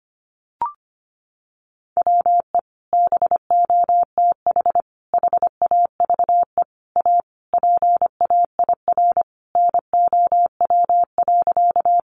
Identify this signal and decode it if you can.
Morse code, heard as a keyed tone, about 25 wpm, 700 Hz. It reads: WE BOTH HAVE A PAIR NOW.
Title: Sentences from Top 1200 Words 25wpm